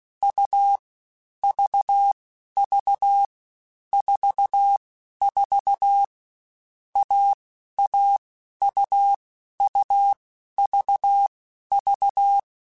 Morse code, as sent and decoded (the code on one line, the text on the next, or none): ..- / ...- / ...- / ....- / ....- / .- / .- / ..- / ..- / ...- / ...-
U V V 4 4 A A U U V V